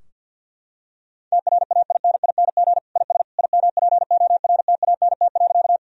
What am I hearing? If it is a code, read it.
Morse code, heard as a keyed tone, about 50 wpm, 700 Hz. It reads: TRAINING IS IMPORTANT.
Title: Sentences from Top 1000 Words - Review Part 4 - 50wpm